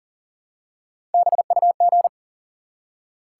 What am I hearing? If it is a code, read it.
Morse code, heard as a keyed tone, about 40 wpm, 700 Hz. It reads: BUG
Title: Three Letter Words - 40wpm